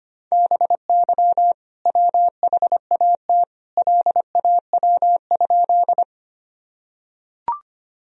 Morse code, heard as a keyed tone, about 25 wpm, 700 Hz. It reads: BY WHAT LAW?
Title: Sentences from Top 1000 Words - Review Part 3 - 25wpm